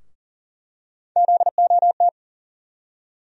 Morse code, heard as a keyed tone, about 40 wpm, 700 Hz. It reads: ZOT
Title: USA Call Sign Postfixes - 3 Letters 40wpm